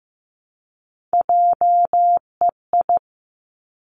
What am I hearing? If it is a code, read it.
Morse code, heard as a keyed tone, about 15 wpm, 700 Hz. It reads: JEI
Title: Three Challenging Letter-Number Combinations 15wpm